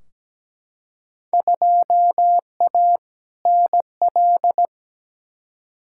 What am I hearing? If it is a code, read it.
Morse code, heard as a keyed tone, about 17 wpm, 700 Hz. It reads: 2A NL
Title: Field Day Exchanges Encore - 17wpm